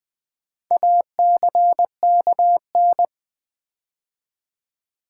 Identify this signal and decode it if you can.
Morse code, heard as a keyed tone, about 20 wpm, 700 Hz. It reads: ACKN
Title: Build A Word - 4 Letters At A Time - 11-Letter Words - 20wpm